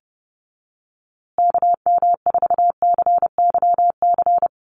Morse code, heard as a keyed tone, about 30 wpm, 700 Hz. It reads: KM4CYC
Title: USA Call Signs Repeated 3x 30wpm